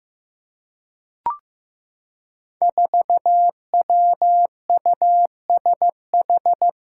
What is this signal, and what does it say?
Morse code, heard as a keyed tone, about 15 wpm, 700 Hz. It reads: 4WUSH
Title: Five Challenging Letter-Number Combinations 15wpm